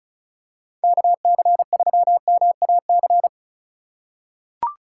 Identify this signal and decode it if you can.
Morse code, heard as a keyed tone, about 36 wpm, 700 Hz. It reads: KC3MAC
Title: USA Call Signs - 6 Characters - 6x Character Spacing - 35wpm